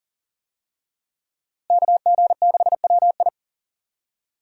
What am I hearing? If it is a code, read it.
Morse code, heard as a keyed tone, about 40 wpm, 700 Hz. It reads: KG6WI
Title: USA Call Signs - 5 Characters - Repeated 3x - 40wpm